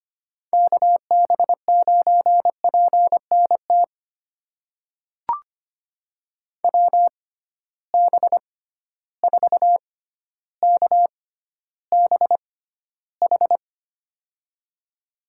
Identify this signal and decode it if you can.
Morse code, heard as a keyed tone, about 25 wpm, spaced as for 13 wpm, 700 Hz. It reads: KB9PNT WB4KBH
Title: USA Call Signs - 6 Characters - 6x Character Spacing - 25wpm